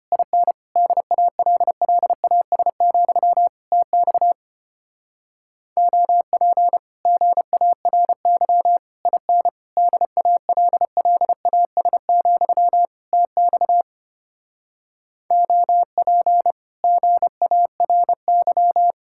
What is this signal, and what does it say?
Morse code, heard as a keyed tone, about 29 wpm, 700 Hz. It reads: IN DALLAS, TX OP GARY IN DALLAS, TX OP GARY
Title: Speed-Racing Common QSO Phrases - 25wpm